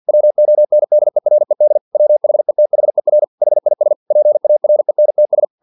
Morse code, heard as a keyed tone, about 50 wpm, 600 Hz. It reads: WONDERED WHETHER HIS PARENTS
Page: Never 49wpm